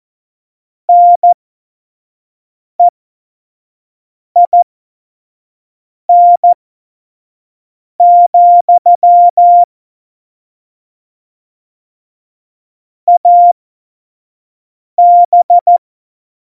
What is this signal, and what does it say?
Morse code, heard as a keyed tone, about 14 wpm, spaced as for 5 wpm, 700 Hz. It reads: NEIN, AB
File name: Morsekurs / Witze / Witz 07. November 2017, 25 F70